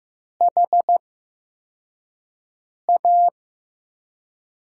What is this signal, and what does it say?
Morse code, heard as a keyed tone, about 15 wpm, spaced as for 4 wpm, 700 Hz. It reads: HA
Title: USA Call Signs - 5 Characters - 8x Character Spacing - 15wpm